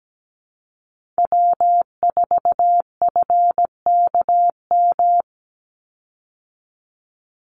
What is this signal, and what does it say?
Morse code, heard as a keyed tone, about 17 wpm, 700 Hz. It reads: W4FKM
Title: Call Signs Repeated 3x - 17wpm